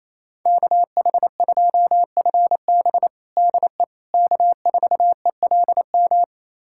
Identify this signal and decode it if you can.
Morse code, heard as a keyed tone, about 28 wpm, 700 Hz. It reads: KH2FB DE K4ELM
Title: USA Callsign DE Callsign - 5 Characters - 28wpm